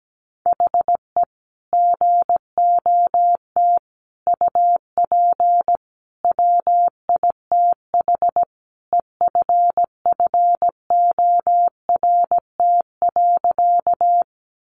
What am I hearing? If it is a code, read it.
Morse code, heard as a keyed tone, about 17 wpm, 700 Hz. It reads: HE GOT UP WITH EFFORT.
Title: Sentences from Top 1200 Words 17wpm